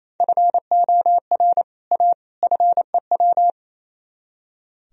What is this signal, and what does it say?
Morse code, heard as a keyed tone, about 34 wpm, 700 Hz. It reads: FOR A FEW
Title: Speed-Racing - Sets of 3 Words 28wpm